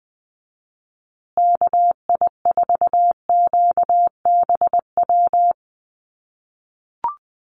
Morse code, heard as a keyed tone, about 20 wpm, 700 Hz. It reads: KI4QBW
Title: USA Call Signs Repeated 2x 20wpm